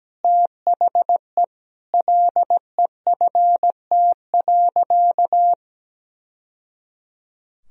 Morse code, heard as a keyed tone, about 17 wpm, 700 Hz. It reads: THE LEFT.
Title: Sentences from Top 1000 Words - Review Part 4 - 17wpm